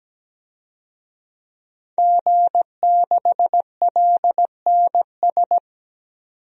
Morse code, heard as a keyed tone, about 17 wpm, 700 Hz. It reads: G6LNS
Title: DXCC-100 Call Signs - 5 Characters - 17wpm